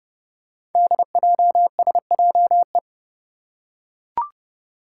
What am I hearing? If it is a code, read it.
Morse code, heard as a keyed tone, about 29 wpm, 700 Hz. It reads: DJSJE
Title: Five Challenging Letter-Number Combinations 30wpm